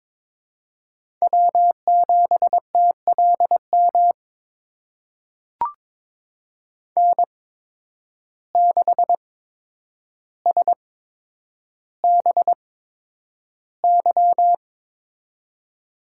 Morse code, heard as a keyed tone, about 22 wpm, spaced as for 9 wpm, 700 Hz. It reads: W7TLM N6SBY
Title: USA Call Signs - 8x Character Spacing - 22wpm